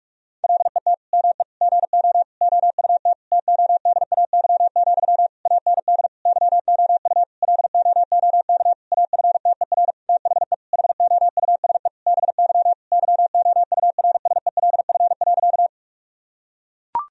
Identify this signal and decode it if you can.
Morse code, heard as a keyed tone, about 41 wpm, 700 Hz. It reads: LET ME GO OUT TODAY, AND YOU LOOK AFTER THE HOUSE BY YOURSELF.
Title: Sentences from Top 1000 Words - Review Part 4 - 45wpm